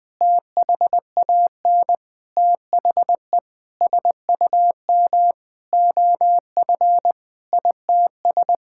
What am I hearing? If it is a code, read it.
Morse code, heard as a keyed tone, about 21 wpm, 700 Hz. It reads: THAN THE SUM OF ITS
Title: Sentences with 6-Word Phrases - 20wpm